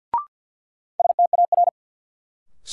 Morse code, heard as a keyed tone, about 53 wpm, 700 Hz. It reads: STAR
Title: Sentences from Top 1000 Words - Review Part 2 - New Word Drill - 50wpm